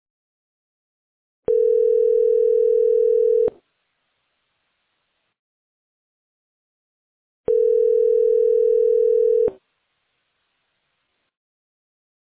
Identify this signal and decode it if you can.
Ringback tone (North America), 440 + 480 Hz.